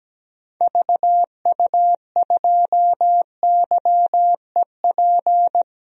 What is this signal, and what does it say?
Morse code, heard as a keyed tone, about 17 wpm, 700 Hz. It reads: VU2YEP
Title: DXCC-100 Call Signs - 6 Characters - Repeated 2x - 17wpm